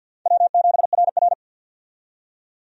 Morse code, heard as a keyed tone, about 49 wpm, 700 Hz. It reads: W7RR
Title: USA Call Signs - 4 Characters - 50wpm